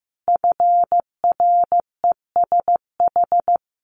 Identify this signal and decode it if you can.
Morse code, heard as a keyed tone, about 15 wpm, 700 Hz. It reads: FRESH
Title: Sentences from Top 1200 Words 15wpm